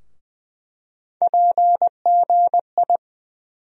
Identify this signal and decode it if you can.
Morse code, heard as a keyed tone, about 20 wpm, 700 Hz. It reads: PGI
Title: USA Call Sign Postfixes - 3 Letters 20wpm